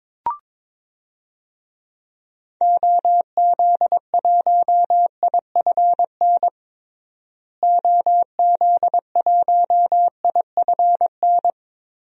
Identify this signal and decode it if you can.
Morse code, heard as a keyed tone, about 21 wpm, 700 Hz. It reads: OZ1IFN OZ1IFN
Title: DXCC-100 Call Signs - 6 Characters - Repeated 3x - 22wpm